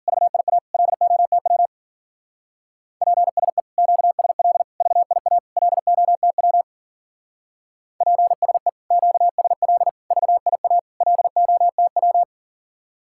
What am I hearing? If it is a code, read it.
Morse code, heard as a keyed tone, about 51 wpm, 700 Hz. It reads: VIA LOTW PSE QSL VIA LOTW PSE QSL VIA LOTW